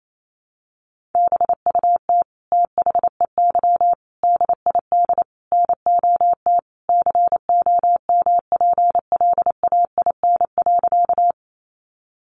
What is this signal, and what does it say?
Morse code, heard as a keyed tone, about 28 wpm, 700 Hz. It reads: BUT THEY DID NOT COMPLAIN.
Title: Sentences from Top 1300 Words 28wpm